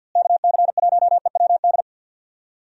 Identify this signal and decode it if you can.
Morse code, heard as a keyed tone, about 52 wpm, 700 Hz. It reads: KC1EWD
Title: USA Call Signs - 6 Characters - Repeated 2x - 50wpm